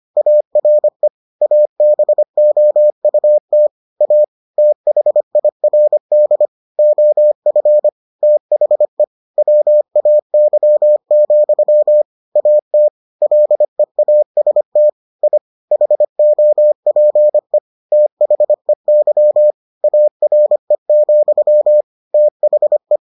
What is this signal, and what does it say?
Morse code, heard as a keyed tone, about 25 wpm, 600 Hz. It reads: ARE ABOUT A THIRD OF THE WAY, AT LEAST I HOPE THEY ARE, THE